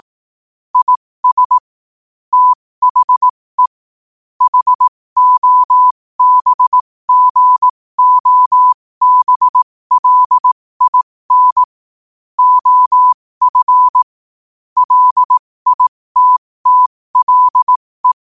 Morse code, heard as a keyed tone, about 18 wpm, spaced as for 15 wpm, 1000 Hz. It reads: IS THE HOBGOBLIN OF LITTLE
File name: qotdfiles / 15wpm